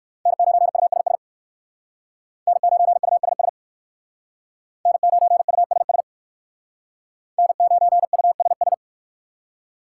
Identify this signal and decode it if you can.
Morse code, heard as a keyed tone, about 59 wpm, 700 Hz. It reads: N9USS N9USS N9USS N9USS